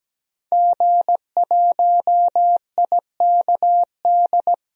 Morse code, heard as a keyed tone, about 17 wpm, 700 Hz. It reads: G1IKD
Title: DXCC-100 Call Signs - 5 Characters - 17wpm